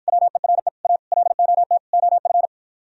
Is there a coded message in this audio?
Yes: Morse code from 0.1 to 2.5 s, heard as a keyed tone, about 52 wpm, 700 Hz. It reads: WERE A LOT OF